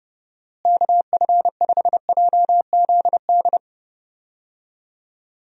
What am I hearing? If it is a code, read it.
Morse code, heard as a keyed tone, about 32 wpm, 700 Hz. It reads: KF5JZD
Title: USA Call Signs - 6 Characters - Repeated 2x - 30wpm